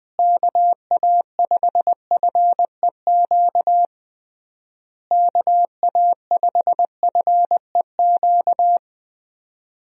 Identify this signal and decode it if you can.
Morse code, heard as a keyed tone, about 20 wpm, 700 Hz. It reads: KA5FEQ KA5FEQ
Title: USA Call Signs - 6 Characters - Repeated 3x - 20wpm